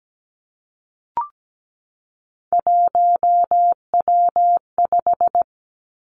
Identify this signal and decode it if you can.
Morse code, heard as a keyed tone, about 17 wpm, 700 Hz. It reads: 1W5
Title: Three Challenging Letter-Number Combinations 17wpm